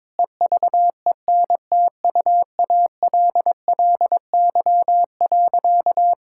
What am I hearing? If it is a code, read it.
Morse code, heard as a keyed tone, about 22 wpm, 700 Hz. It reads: EVENTUALLY.
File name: practice / 22wpm / Sentences with 8-Word Phrases - 22wpm